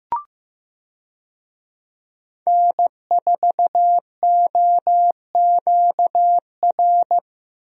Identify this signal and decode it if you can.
Morse code, heard as a keyed tone, about 16 wpm, 700 Hz. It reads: N4OQR
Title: USA Call Signs - 5 Characters - Repeated 2x - 15wpm